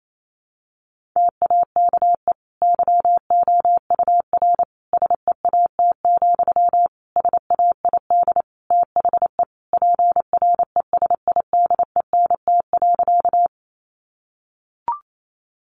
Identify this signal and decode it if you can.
Morse code, heard as a keyed tone, about 28 wpm, 700 Hz. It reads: TAKE YOUR SEAT, SAID THE PRESIDENT.
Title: Sentences from Top 1500 Words 28wpm